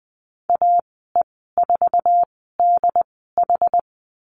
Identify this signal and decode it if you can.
Morse code, heard as a keyed tone, about 20 wpm, spaced as for 14 wpm, 700 Hz. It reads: AE4DH
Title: 069 Lesson 12g - Letter d - Call Signs 20wpm